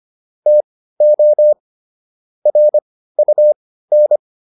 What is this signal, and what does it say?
Morse code, heard as a keyed tone, about 25 wpm, spaced as for 15 wpm, 600 Hz. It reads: TO RUN